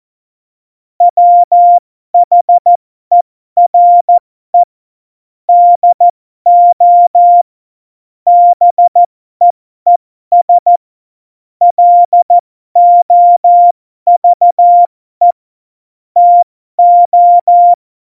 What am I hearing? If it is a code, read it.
Morse code, heard as a keyed tone, about 14 wpm, spaced as for 12 wpm, 700 Hz. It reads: WHERE DO BEES LOVE TO
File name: Morsekurs / Witze